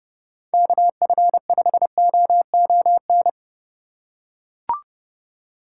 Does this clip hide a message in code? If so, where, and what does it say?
Yes: Morse code from 0.5 to 3.3 s, heard as a keyed tone, about 30 wpm, 700 Hz. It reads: KF5OON